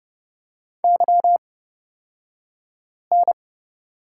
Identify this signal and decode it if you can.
Morse code, heard as a keyed tone, about 30 wpm, spaced as for 5 wpm, 700 Hz. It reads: YN
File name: Lesson 38d - Letter q - Two Characters 30wpm